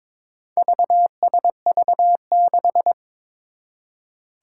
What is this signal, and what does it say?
Morse code, heard as a keyed tone, about 21 wpm, 700 Hz. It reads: VSV6
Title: Four Challenging Letter-Number Combinations 22wpm